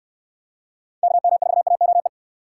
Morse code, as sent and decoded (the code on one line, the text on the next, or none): -.- -- ....- .- --- .
KM4AOE